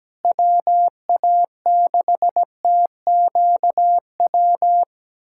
Morse code, heard as a keyed tone, about 17 wpm, 700 Hz. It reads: WA6TQW
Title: USA Callsign DE Callsign - 6 Characters - 17wpm